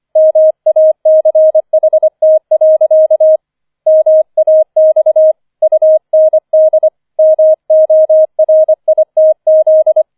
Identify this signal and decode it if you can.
Morse code, heard as a keyed tone, about 25 wpm, 630 Hz. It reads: MACHT. MAX UND MORITZ
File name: 005 Fünfter Streich 25wpm